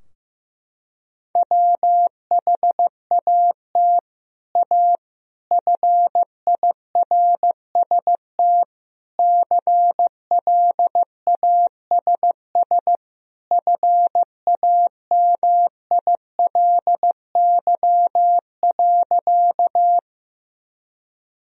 Morse code, heard as a keyed tone, about 15 wpm, 700 Hz. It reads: WHAT A FIRST CLASS FAMILY.
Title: Sentences from Top 1000 Words - Review Part 2 - 15wpm